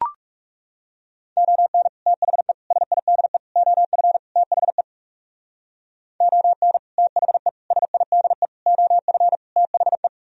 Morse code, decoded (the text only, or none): ON THE SIDE OF THE ON THE SIDE OF THE